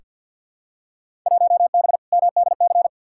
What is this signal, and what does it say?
Morse code, heard as a keyed tone, about 46 wpm, 700 Hz. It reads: 1B MDC
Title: Field Day Exchanges - 50wpm